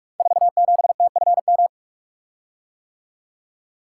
Morse code, heard as a keyed tone, about 44 wpm, 700 Hz. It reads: 47TFM
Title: Code Groups - 5 Letter:Numbers - 45wpm